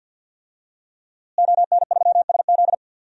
Morse code, heard as a keyed tone, about 52 wpm, 700 Hz. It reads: ON3SZ
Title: DXCC-100 Call Signs - 5 Characters - Repeated 3x - 50wpm